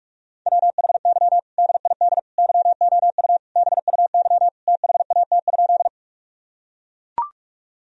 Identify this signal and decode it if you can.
Morse code, heard as a keyed tone, about 41 wpm, 700 Hz. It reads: WHY DID YOU BUY THAT?
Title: Sentences from Top 1000 Words - Review Part 4 - 45wpm